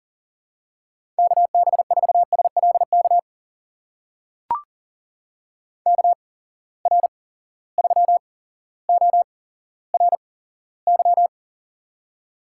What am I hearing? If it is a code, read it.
Morse code, heard as a keyed tone, about 42 wpm, spaced as for 17 wpm, 700 Hz. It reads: KB4SLK KR3ORY
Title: USA Call Signs - 6 Characters - 8x Character Spacing - 40wpm